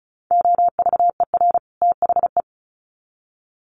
Morse code, heard as a keyed tone, about 34 wpm, 700 Hz. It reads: OVER THE
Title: Sets of 2 Words 35wpm